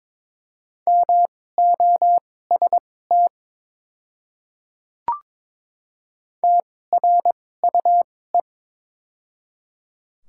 Morse code, heard as a keyed tone, about 22 wpm, spaced as for 16 wpm, 700 Hz. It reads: MOST TRUE
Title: 2x Character Spacing - Top 500 Words - 22wpm